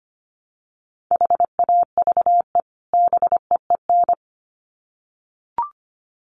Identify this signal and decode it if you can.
Morse code, heard as a keyed tone, about 25 wpm, 700 Hz. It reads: HAVE BEEN
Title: Sets of 2 Words 25wpm